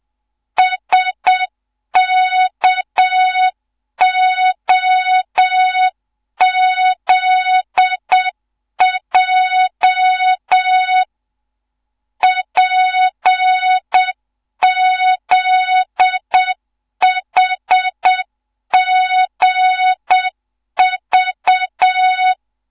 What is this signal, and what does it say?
Morse code, heard as a keyed tone, about 7 wpm, 3700 Hz, with noise. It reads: SKOZJ PZHGV
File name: Morze-HA2MN / hang / 15 GYAKORLAT